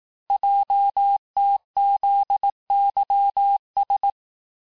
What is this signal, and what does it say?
Morse code, heard as a keyed tone, about 18 wpm, 800 Hz. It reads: JTZYS
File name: cours-du-lundi files / cours-1852-texte-chiffre-vitesse-17MPM